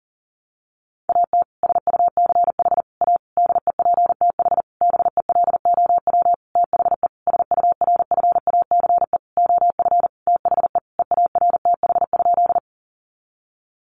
Morse code, heard as a keyed tone, about 40 wpm, 700 Hz. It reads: AT SUCH A DEPTH BELOW THE SURFACE OF THE EARTH?
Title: Sentences from Top 1200 Words 40wpm